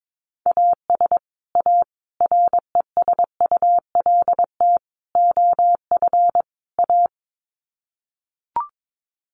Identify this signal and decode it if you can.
Morse code, heard as a keyed tone, about 22 wpm, 700 Hz. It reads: AS A RESULT OF A
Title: Sets of 5 Words Encore 22wpm